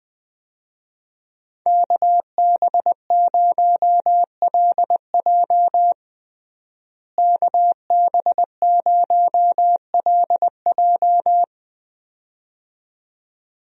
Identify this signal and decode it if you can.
Morse code, heard as a keyed tone, about 20 wpm, 700 Hz. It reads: KB0LJ KB0LJ
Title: USA Call Signs - 5 Characters - Repeated 2x - 20wpm